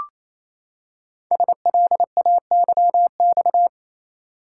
Morse code, heard as a keyed tone, about 27 wpm, 700 Hz. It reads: SLAYX
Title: Code Groups - 5 Letters - 28wpm